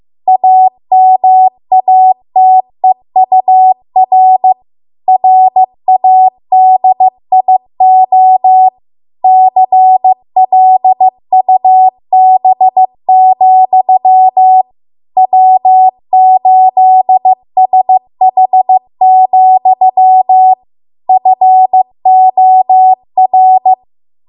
Morse code, heard as a keyed tone, about 15 wpm, 750 Hz. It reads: AMATEUR RADIO CLUB, W8SH, FOR
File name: Archive / 15 WPM / 15WPM